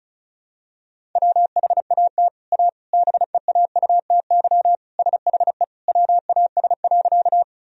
Morse code, heard as a keyed tone, about 34 wpm, 700 Hz. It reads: WHAT A BEAUTY SHE WAS.
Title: Sentences from Top 1000 Words - Review Part 2 - 35wpm